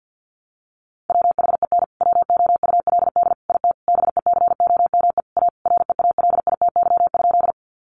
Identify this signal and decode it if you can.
Morse code, heard as a keyed tone, about 50 wpm, 700 Hz. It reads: WHEN WOULD IT BECOME A REALITY?
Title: Sentences from Top 1200 Words 50wpm